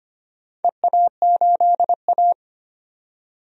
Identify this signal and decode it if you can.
Morse code, heard as a keyed tone, about 25 wpm, 700 Hz. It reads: EA8A
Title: DXCC-100 Call Signs - 4 Characters - 25wpm